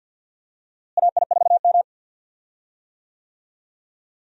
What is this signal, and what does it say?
Morse code, heard as a keyed tone, about 46 wpm, 700 Hz. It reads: AI4M
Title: USA Call Signs - 4 Characters - Repeated 3x - 50wpm